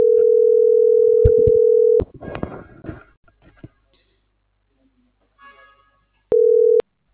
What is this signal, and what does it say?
Ringback tone (North America), 440 + 480 Hz.